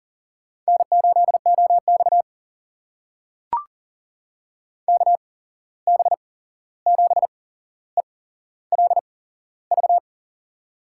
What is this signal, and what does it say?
Morse code, heard as a keyed tone, about 40 wpm, spaced as for 15 wpm, 700 Hz. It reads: N8OX KB7ELV
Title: USA Call Signs - 8x Character Spacing - 40wpm